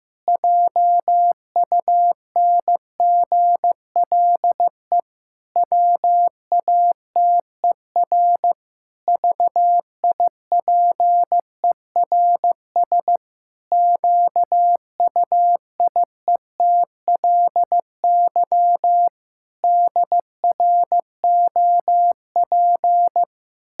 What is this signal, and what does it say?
Morse code, heard as a keyed tone, about 15 wpm, 700 Hz. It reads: JUNGLE WATER VIPERS QUIETLY DROP
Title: Pangram Challenge v2 - 15wpm